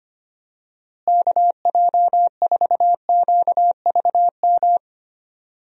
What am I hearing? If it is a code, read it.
Morse code, heard as a keyed tone, about 25 wpm, 700 Hz. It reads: KJ4QVM
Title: USA Call Signs - 6 Characters - Repeated 2x - 25wpm